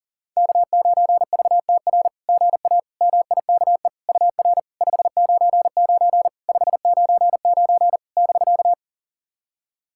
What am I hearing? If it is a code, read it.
Morse code, heard as a keyed tone, about 44 wpm, 700 Hz. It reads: K9VTR GA MIKE UR 599 599 <BK>
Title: SOTA QSO Practice 40wpm